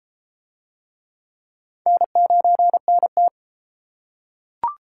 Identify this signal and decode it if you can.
Morse code, heard as a keyed tone, about 32 wpm, 700 Hz. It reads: N9NT